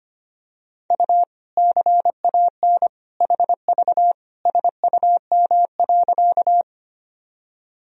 Morse code, heard as a keyed tone, about 25 wpm, 700 Hz. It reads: U CAN HV SUM.
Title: Sentences from Top 100 Words and Abbreviations 25wpm